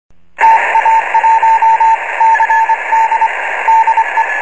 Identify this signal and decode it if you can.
Morse code, heard as a keyed tone, about 25 wpm, 900 Hz, with radio noise. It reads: TA1CD DE